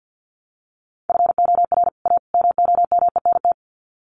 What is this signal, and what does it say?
Morse code, heard as a keyed tone, about 49 wpm, 700 Hz. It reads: FOR A MOMENT
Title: Sets of 3 Words 50wpm